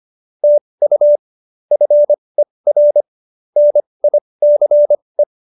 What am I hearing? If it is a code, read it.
Morse code, heard as a keyed tone, about 25 wpm, spaced as for 20 wpm, 590 Hz. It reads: TU FER NICE
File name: QSO310 20